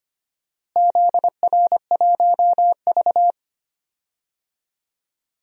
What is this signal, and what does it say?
Morse code, heard as a keyed tone, about 25 wpm, 700 Hz. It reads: ZR1V
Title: DXCC-100 Call Signs - 4 Characters - Repeated 3x - 25wpm